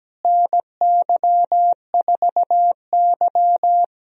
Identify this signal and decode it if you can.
Morse code, heard as a keyed tone, about 17 wpm, 700 Hz. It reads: NY4Y
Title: USA Call Signs - 4 Characters - Repeated 2x - 17wpm